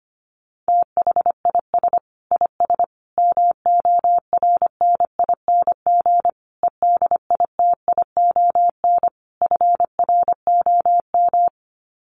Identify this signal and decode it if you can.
Morse code, heard as a keyed tone, about 25 wpm, 700 Hz. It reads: THIS IS MORNING EDITION FROM
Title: Sets of 5 Words Encore2 25wpm